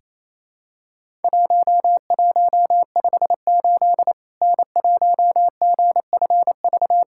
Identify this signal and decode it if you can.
Morse code, heard as a keyed tone, about 28 wpm, 700 Hz. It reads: 1158 N1GFV
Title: Worked All Europe DX Contest - QTC Exchange - 28wpm